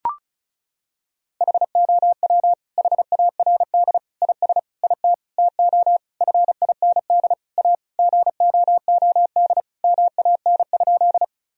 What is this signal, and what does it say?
Morse code, heard as a keyed tone, about 34 wpm, 700 Hz. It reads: HOW HARD IS IT TO FIND A GOOD MAN?
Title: Sentences from Top 1000 Words - Review Part 1 - 35wpm